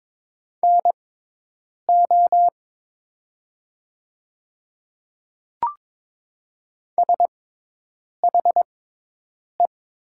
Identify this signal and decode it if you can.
Morse code, heard as a keyed tone, about 22 wpm, spaced as for 8 wpm, 700 Hz. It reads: NO SHE
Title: 6x Character Spacing - Top 100 Words - 22wpm